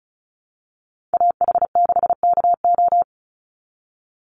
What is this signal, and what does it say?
Morse code, heard as a keyed tone, about 34 wpm, 700 Hz. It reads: AH6KO
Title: Call Signs Repeated 3x - 35wpm